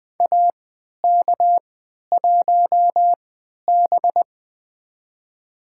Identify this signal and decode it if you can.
Morse code, heard as a keyed tone, about 20 wpm, spaced as for 11 wpm, 700 Hz. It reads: AK1B
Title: USA Call Signs - 4 Characters - 3x Character Spacing - 20wpm